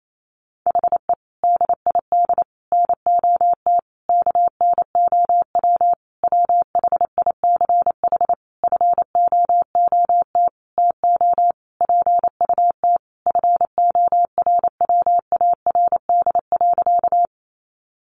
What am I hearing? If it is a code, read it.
Morse code, heard as a keyed tone, about 28 wpm, 700 Hz. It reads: HE DID NOT KNOW WHICH FOOT TO PUT FORWARD.
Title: Sentences from Top 1300 Words 28wpm